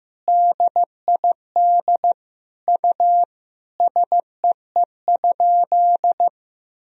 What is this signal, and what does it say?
Morse code, heard as a keyed tone, about 15 wpm, 700 Hz. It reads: DID U SEE?
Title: Sentences from Top 100 Words and Abbreviations 15wpm